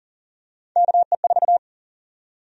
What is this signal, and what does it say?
Morse code, heard as a keyed tone, about 40 wpm, 700 Hz. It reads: KE4
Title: USA Call Sign Prefixes - 40wpm